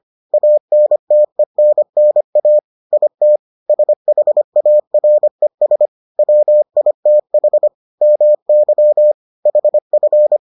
Morse code, heard as a keyed tone, about 25 wpm, 600 Hz. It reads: ANTENNA IT SHARES WITH MY HF